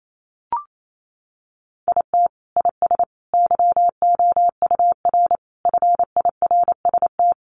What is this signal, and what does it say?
Morse code, heard as a keyed tone, about 28 wpm, 700 Hz. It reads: IT IS YOUR FIRST
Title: Sentences from Top 1100 Words 28wpm